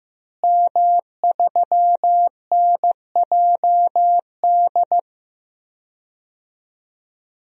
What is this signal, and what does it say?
Morse code, heard as a keyed tone, about 15 wpm, 700 Hz. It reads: M3NJD
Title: DXCC-100 Call Signs - 5 Characters - Repeated 2x - 15wpm